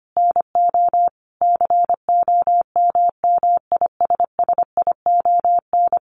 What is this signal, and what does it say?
Morse code, heard as a keyed tone, about 25 wpm, 700 Hz. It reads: NO COMMISSION
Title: Sentences from Top 1400 Words 25wpm